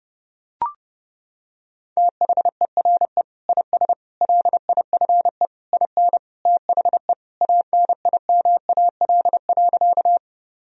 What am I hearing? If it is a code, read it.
Morse code, heard as a keyed tone, about 32 wpm, 700 Hz. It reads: THERE IS LIFE IN THE ANIMAL.
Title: Sentences from Top 1000 Words - Review Part 1 - 30wpm